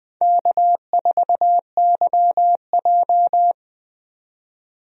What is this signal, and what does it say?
Morse code, heard as a keyed tone, about 20 wpm, 700 Hz. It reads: K4YJ
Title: USA Call Signs - 4 Characters - 20wpm